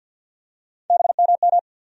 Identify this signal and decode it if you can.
Morse code, heard as a keyed tone, about 52 wpm, 700 Hz. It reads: BMM
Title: USA Call Sign Postfixes - 3 Letters 50wpm